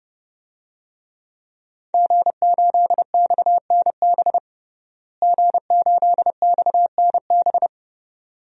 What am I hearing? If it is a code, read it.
Morse code, heard as a keyed tone, about 30 wpm, 700 Hz. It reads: G8XNB G8XNB
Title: DXCC-100 Call Signs - 5 Characters - Repeated 2x - 30wpm